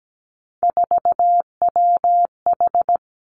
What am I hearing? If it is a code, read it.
Morse code, heard as a keyed tone, about 17 wpm, 700 Hz. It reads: 4WH
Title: Three Challenging Letter-Number Combinations 17wpm